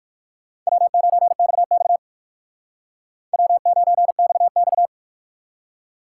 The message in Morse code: W9XX W9XX

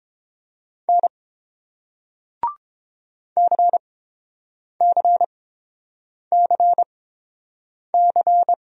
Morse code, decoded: N CCCC

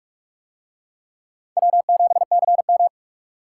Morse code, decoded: W7CM

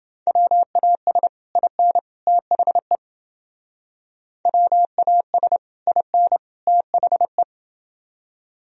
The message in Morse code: WAS IN THE WAS IN THE

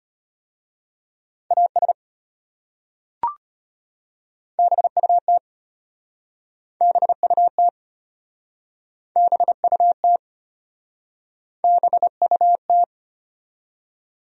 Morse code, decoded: AS BUT BUT BUT BUT

